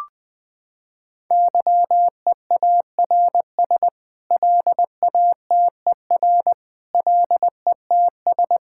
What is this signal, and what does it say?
Morse code, heard as a keyed tone, about 20 wpm, 700 Hz. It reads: YEARS LATER LETS
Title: Sets of 8 Words - 20wpm